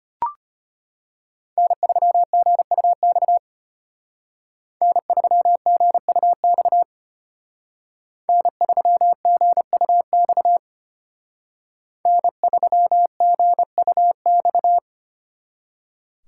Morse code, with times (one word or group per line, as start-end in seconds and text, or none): 1.6-3.4 N3GUX
4.8-6.8 N3GUX
8.3-10.6 N3GUX
12.1-14.8 N3GUX